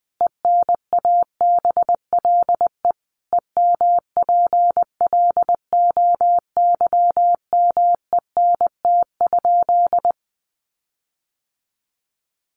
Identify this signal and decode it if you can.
Morse code, heard as a keyed tone, about 20 wpm, 700 Hz. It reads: ENABLE EMPLOYMENT?
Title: Sentences from Top 1500 Words 20wpm